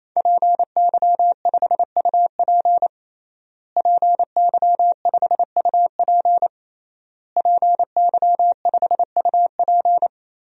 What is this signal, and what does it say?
Morse code, heard as a keyed tone, about 27 wpm, 700 Hz. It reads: PY5UP PY5UP PY5UP
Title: DXCC-100 Call Signs - 5 Characters - Repeated 3x - 28wpm